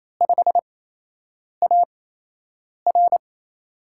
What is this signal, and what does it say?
Morse code, heard as a keyed tone, about 28 wpm, spaced as for 8 wpm, 700 Hz. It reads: 5AR
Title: USA Call Signs - 5 Characters - 8x Character Spacing - 28wpm